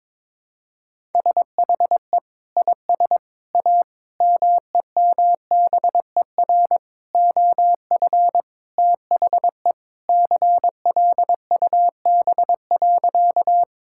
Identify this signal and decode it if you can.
Morse code, heard as a keyed tone, about 22 wpm, 700 Hz. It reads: SHE IS A MEMBER OF THE CLUB.
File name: practice / 22wpm / Sentences with 4-Word Phrases - 22wpm